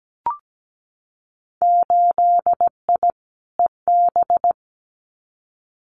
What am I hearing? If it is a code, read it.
Morse code, heard as a keyed tone, about 17 wpm, 700 Hz. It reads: 8I EB
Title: Winter Field Day Exchanges Encore 17wpm